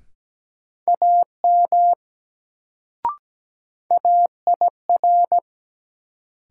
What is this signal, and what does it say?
Morse code, heard as a keyed tone, about 17 wpm, 700 Hz. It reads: AM AIR
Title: Sets of 2 Words - New Word Drill - 17wpm